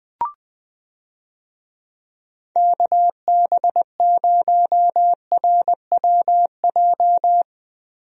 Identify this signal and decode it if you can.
Morse code, heard as a keyed tone, about 20 wpm, 700 Hz. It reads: KB0RWJ
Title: USA Call Signs - 6 Characters - Repeated 2x - 20wpm